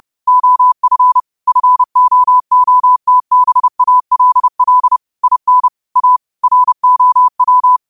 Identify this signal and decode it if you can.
Morse code, heard as a keyed tone, about 30 wpm, 1000 Hz. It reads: OF FOOTBALL IN A ROW
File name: qotdfiles / 30wpm